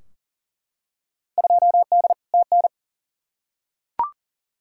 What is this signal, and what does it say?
Morse code, heard as a keyed tone, about 40 wpm, 700 Hz. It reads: 2D TN
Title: Field Day Exchanges Encore - 40wpm